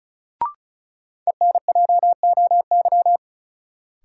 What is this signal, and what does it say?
Morse code, heard as a keyed tone, about 36 wpm, 700 Hz. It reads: ENJOY
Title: Sentences from Top 1000 Words - Review Part 4 - New Word Drill - 35wpm